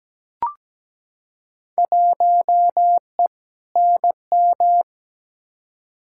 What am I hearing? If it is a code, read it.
Morse code, heard as a keyed tone, about 17 wpm, 700 Hz. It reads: 1E NM
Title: Field Day Exchanges - 17wpm